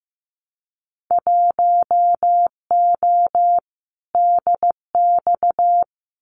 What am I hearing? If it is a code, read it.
Morse code, heard as a keyed tone, about 15 wpm, 700 Hz. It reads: 1O DX
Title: Winter Field Day Exchanges Encore 15wpm